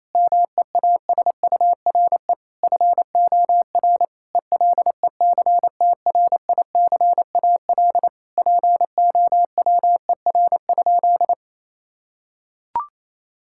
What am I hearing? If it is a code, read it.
Morse code, heard as a keyed tone, about 28 wpm, 700 Hz. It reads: MEASURE FOR ELECTRICAL POWER?
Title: Sentences from Top 1000 Words - Review Part 6 - 28wpm